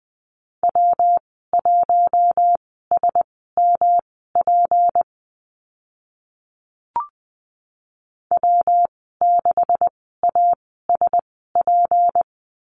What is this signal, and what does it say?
Morse code, heard as a keyed tone, about 20 wpm, spaced as for 14 wpm, 700 Hz. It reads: W1SMP W6ASP
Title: 238 Lesson 38g - Letter q - Call Signs 20wpm